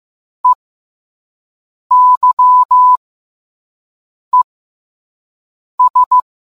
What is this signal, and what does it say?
Morse code, heard as a keyed tone, about 15 wpm, spaced as for 5 wpm, 1000 Hz. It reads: EYES